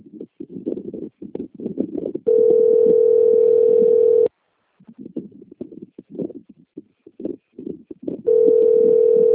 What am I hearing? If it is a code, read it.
Ringback tone (North America), 440 + 480 Hz.